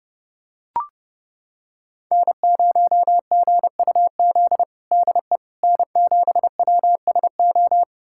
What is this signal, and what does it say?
Morse code, heard as a keyed tone, about 32 wpm, 700 Hz. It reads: N0GUZ DE N7WSO
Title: USA Callsign DE Callsign - 5 Characters - 30wpm